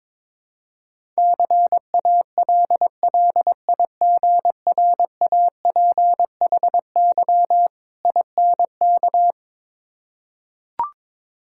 Morse code, heard as a keyed tone, about 23 wpm, 690 Hz. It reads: CALLIGRAPHY INK
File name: Difficult Words with Hints - 22wpm